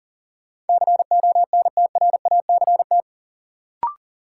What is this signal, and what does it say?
Morse code, heard as a keyed tone, about 40 wpm, 700 Hz. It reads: CONTRACT
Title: Sentences from Top 1400 Words - New Word Drill Alt - 40wpm